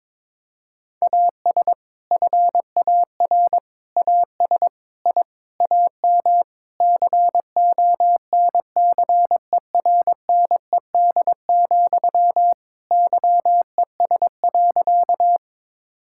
Morse code, heard as a keyed tone, about 22 wpm, 700 Hz. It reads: AS FAR AS I AM CONCERNED, YES.
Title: Sentences from Top 1000 Words - Review Part 6 - 22wpm